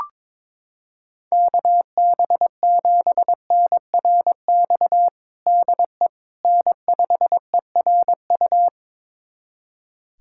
Morse code, heard as a keyed tone, about 22 wpm, 700 Hz. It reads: KB7NRX DE N5ERU
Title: USA Callsign DE Callsign 22wpm